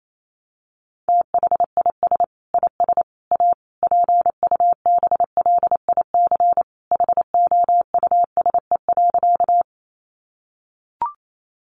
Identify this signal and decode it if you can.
Morse code, heard as a keyed tone, about 28 wpm, 700 Hz. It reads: THIS IS A PUBLIC HOUSE.
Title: Sentences from Top 600 Words 28wpm